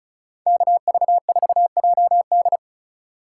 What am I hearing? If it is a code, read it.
Morse code, heard as a keyed tone, about 44 wpm, 700 Hz. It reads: KV4JD